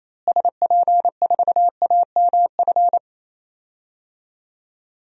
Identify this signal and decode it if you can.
Morse code, heard as a keyed tone, about 28 wpm, 700 Hz. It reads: SP4AMF
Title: DXCC-100 Call Signs - 6 Characters - Repeated 2x - 28wpm